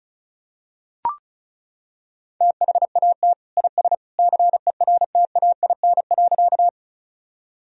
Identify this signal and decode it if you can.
Morse code, heard as a keyed tone, about 34 wpm, 700 Hz. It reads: THAT IS CERTAIN.
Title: Sentences from Top 1000 Words - Review Part 2 - 35wpm